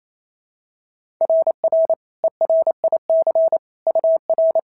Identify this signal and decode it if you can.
Morse code, heard as a keyed tone, about 25 wpm, 650 Hz. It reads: RR ERIC UR
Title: SOTA QSO Practice 28wpm